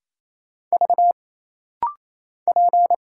Morse code, heard as a keyed tone, about 28 wpm, 700 Hz. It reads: V P
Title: Single Letter-Number Alt - Rapid-Fire 28wpm